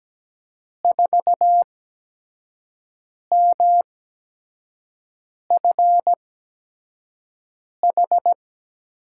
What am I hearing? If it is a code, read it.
Morse code, heard as a keyed tone, about 17 wpm, 700 Hz. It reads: 4 M F H
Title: USA Call Signs - 6 Characters - 8x Character Spacing - 17wpm